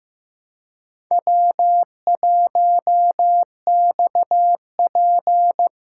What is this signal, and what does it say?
Morse code, heard as a keyed tone, about 15 wpm, 700 Hz. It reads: W1XP
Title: USA Callsign DE Callsign - 4 Characters - 15wpm